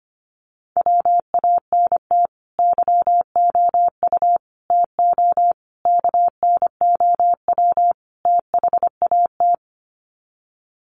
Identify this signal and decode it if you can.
Morse code, heard as a keyed tone, about 25 wpm, 700 Hz. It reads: WANT YOU TO KNOW THAT
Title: Sets of 5 Words Encore 25wpm